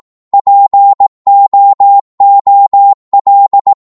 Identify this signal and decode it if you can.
Morse code, heard as a keyed tone, about 18 wpm, 800 Hz. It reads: POOL
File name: Morse / psg-18wpm